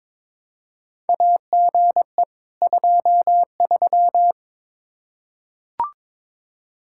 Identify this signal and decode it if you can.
Morse code, heard as a keyed tone, about 22 wpm, 700 Hz. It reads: AGE 23
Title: K1USN SST QSO Practice 22wpm